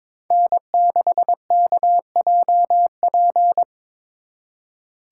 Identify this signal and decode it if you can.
Morse code, heard as a keyed tone, about 25 wpm, 700 Hz. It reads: N6KJP
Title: Speed Racing - USA 1x3 Call Signs 22wpm